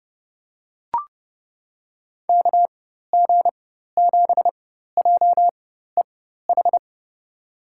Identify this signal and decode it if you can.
Morse code, heard as a keyed tone, about 32 wpm, spaced as for 14 wpm, 700 Hz. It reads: KG7JEH
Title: USA Call Signs - 6 Characters - 4x Character Spacing - 30wpm